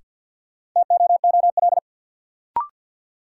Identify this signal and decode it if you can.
Morse code, heard as a keyed tone, about 53 wpm, 700 Hz. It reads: TOOL
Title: Sentences from Top 1000 Words - Review Part 6 - New Word Drill - 50wpm